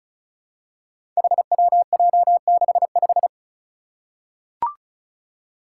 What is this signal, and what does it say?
Morse code, heard as a keyed tone, about 36 wpm, 700 Hz. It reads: HWJ65
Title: Five Challenging Letter-Number Combinations 35wpm